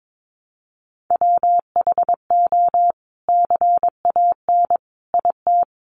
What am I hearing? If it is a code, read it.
Morse code, heard as a keyed tone, about 22 wpm, 700 Hz. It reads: WHO CAN IT
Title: Sentences from Top 100 Words 22wpm